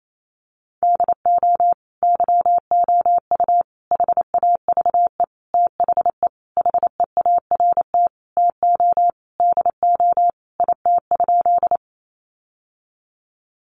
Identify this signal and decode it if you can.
Morse code, heard as a keyed tone, about 28 wpm, 700 Hz. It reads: DO YOU HAVE THE HEART TO DO IT?
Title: Sentences from Top 1200 Words 28wpm